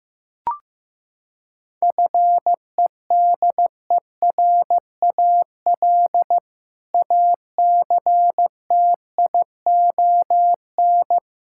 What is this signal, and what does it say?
Morse code, heard as a keyed tone, about 16 wpm, 700 Hz. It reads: FEDERAL ACTION
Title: Sentences from Top 1000 Words - Review Part 5 - 15wpm